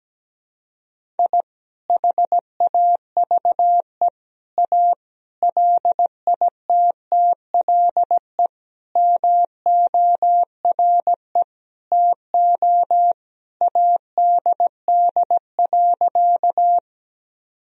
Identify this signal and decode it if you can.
Morse code, heard as a keyed tone, about 17 wpm, 700 Hz. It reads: I HAVE A LITTLE MORE TO ADD.
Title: Sentences from Top 1000 Words - Review Part 1 - 17wpm